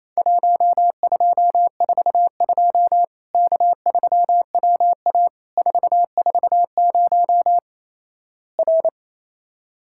Morse code, heard as a keyed tone, about 27 wpm, 700 Hz. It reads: 1242 K3WA 440 R
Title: Worked All Europe DX Contest - QTC Exchange - 28wpm